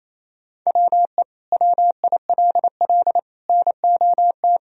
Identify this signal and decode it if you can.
Morse code, heard as a keyed tone, about 28 wpm, 700 Hz. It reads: WE WILL NOT
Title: Sentences from Top 1000 Words - Review Part 2 - 28wpm